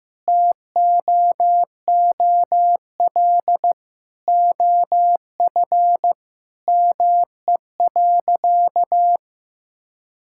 Morse code, heard as a keyed tone, about 15 wpm, 700 Hz. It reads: TOOL OF ME.
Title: Sentences from Top 1000 Words - Review Part 6 - 15wpm